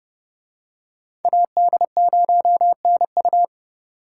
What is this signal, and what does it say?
Morse code, heard as a keyed tone, about 30 wpm, 700 Hz. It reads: AD0NU